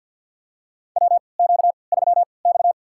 Morse code, heard as a keyed tone, about 53 wpm, spaced as for 29 wpm, 700 Hz. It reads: WQ3X
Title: USA Call Signs - 4 Characters - 3x Character Spacing - 50wpm